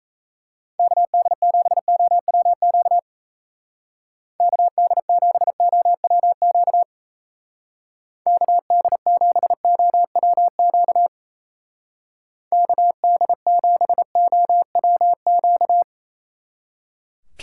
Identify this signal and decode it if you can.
Morse code, heard as a keyed tone, about 34 wpm, 700 Hz. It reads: KD7OWQ KD7OWQ KD7OWQ KD7OWQ
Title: Speed Racing - USA 2x3 Call Signs 28wpm